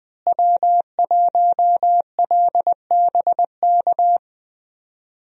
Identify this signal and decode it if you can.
Morse code, heard as a keyed tone, about 20 wpm, 700 Hz. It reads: W1LBK
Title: USA Call Signs - 5 Characters - Repeated 3x - 20wpm